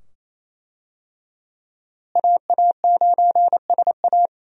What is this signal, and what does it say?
Morse code, heard as a keyed tone, about 28 wpm, 700 Hz. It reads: AA9SA
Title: USA Call Signs - 5 Characters - Repeated 2x - 28wpm